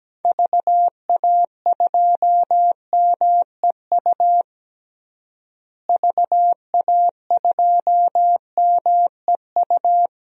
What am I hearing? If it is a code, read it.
Morse code, heard as a keyed tone, about 17 wpm, 700 Hz. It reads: VA2MEU VA2MEU
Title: DXCC-100 Call Signs - 6 Characters - Repeated 3x - 17wpm